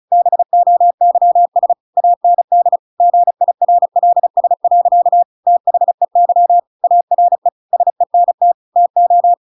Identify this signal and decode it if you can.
Morse code, heard as a keyed tone, about 36 wpm, 700 Hz. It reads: BOYS AND GIRLS. THEY ARE SENT TO